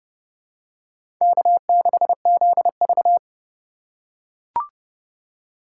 Morse code, heard as a keyed tone, about 32 wpm, 700 Hz. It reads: K6ZV
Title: USA Call Signs - 4 Characters - 2x Character Spacing - 30wpm